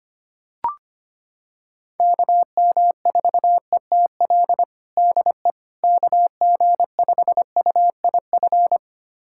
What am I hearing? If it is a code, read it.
Morse code, heard as a keyed tone, about 25 wpm, 700 Hz. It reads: KM4ETL DE KG5UIF
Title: USA Callsign DE Callsign - 6 Characters - 25wpm